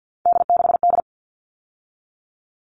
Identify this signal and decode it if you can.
Morse code, heard as a keyed tone, about 50 wpm, 700 Hz. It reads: D6D
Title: All Cut Numbers - 3-Digit 50wpm